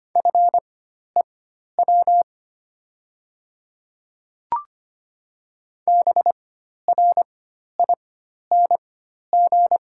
Morse code, heard as a keyed tone, about 25 wpm, spaced as for 12 wpm, 700 Hz. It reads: FEW BRING
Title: 4x Character Spacing - Top 500 Words - 25wpm